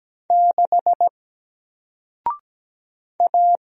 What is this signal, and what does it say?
Morse code, heard as a keyed tone, about 17 wpm, 700 Hz. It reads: 6 A
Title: Single Character (Letter:Number:Punctuation:Prosign) 17wpm